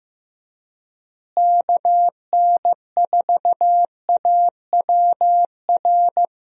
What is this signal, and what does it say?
Morse code, heard as a keyed tone, about 15 wpm, 700 Hz. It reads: KN4AWR
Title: USA Call Signs - 6 Characters - Repeated 2x - 15wpm